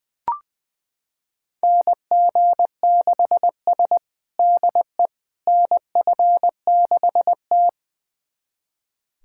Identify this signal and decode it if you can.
Morse code, heard as a keyed tone, about 20 wpm, 700 Hz. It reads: NG6S DE NF6T
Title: USA Callsign DE Callsign - 4 Characters - 20wpm